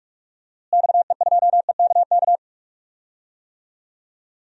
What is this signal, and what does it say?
Morse code, heard as a keyed tone, about 46 wpm, 700 Hz. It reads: XE2EKK
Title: DXCC-100 Call Signs - 6 Characters - Repeated 2x - 45wpm